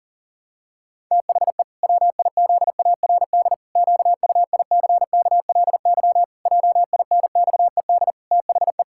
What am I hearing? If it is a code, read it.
Morse code, heard as a keyed tone, about 40 wpm, 700 Hz. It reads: THE WIZARD QUICKLY JINXED THE
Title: Pangram Challenge v2 - 40wpm